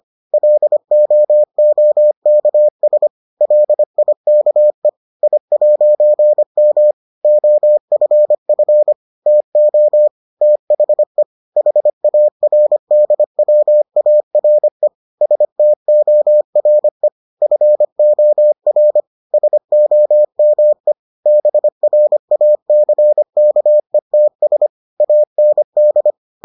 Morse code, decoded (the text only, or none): LOOKS LIKE I'M OFF TO THE HARDWARE STORE FOR SOME BRACKETS AND